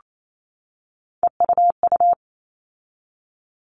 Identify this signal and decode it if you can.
Morse code, heard as a keyed tone, about 28 wpm, 700 Hz. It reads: EUU
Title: All Cut Numbers - 3-Digit 28wpm